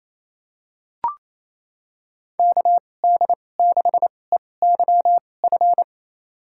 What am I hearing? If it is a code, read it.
Morse code, heard as a keyed tone, about 27 wpm, spaced as for 20 wpm, 700 Hz. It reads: KD6EYF
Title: USA Call Signs - 6 Characters - 2x Character Spacing - 28wpm